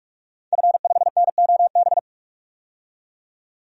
Morse code, heard as a keyed tone, about 46 wpm, 700 Hz. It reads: F5NOB
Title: DXCC-100 Call Signs - 5 Characters - 45wpm